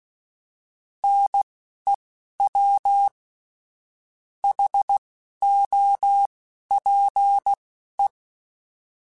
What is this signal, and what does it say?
Morse code, heard as a keyed tone, about 16 wpm, spaced as for 11 wpm, 800 Hz. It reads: NEW HOPE
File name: fromSMs / FromSMs mp3s / Cities@10